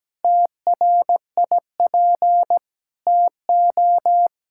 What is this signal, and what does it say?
Morse code, heard as a keyed tone, about 17 wpm, 700 Hz. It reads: TRIP TO
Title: Pangram Challenge v2 - 17wpm